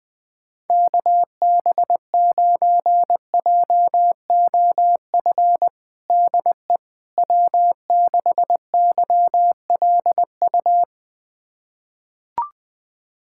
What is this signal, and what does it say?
Morse code, heard as a keyed tone, about 21 wpm, 700 Hz. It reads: KB9JOF DE W6YLU
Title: USA Callsign DE Callsign 20wpm